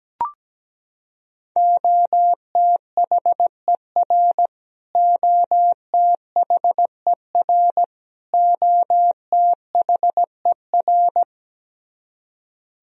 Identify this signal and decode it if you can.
Morse code, heard as a keyed tone, about 17 wpm, 700 Hz. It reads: OTHER OTHER OTHER
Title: Top 100 Words Repeated 17wpm